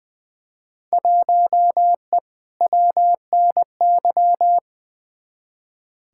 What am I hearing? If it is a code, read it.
Morse code, heard as a keyed tone, about 20 wpm, 700 Hz. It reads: 1E WNY
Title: Field Day Exchanges - 20wpm